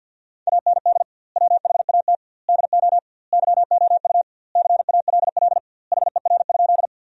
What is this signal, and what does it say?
Morse code, heard as a keyed tone, about 50 wpm, 700 Hz. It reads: AND WHAT DO YOU CALL HER?
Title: 2x Word-Spacing - Sentences from Top 100 Words - 50wpm